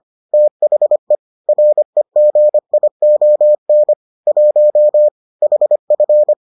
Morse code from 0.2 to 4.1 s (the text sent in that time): THE REGION